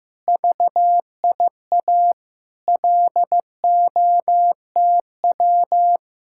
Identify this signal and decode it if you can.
Morse code, heard as a keyed tone, about 15 wpm, 700 Hz. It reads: VIA LOTW
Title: Speed-Racing Common QSO Phrases Encore - 15wpm